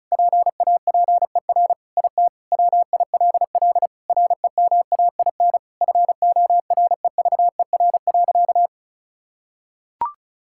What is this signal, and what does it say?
Morse code, heard as a keyed tone, about 36 wpm, 700 Hz. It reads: PAPER IT WILL REMAIN FOREVER.
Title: Sentences from Top 1600 Words 35wpm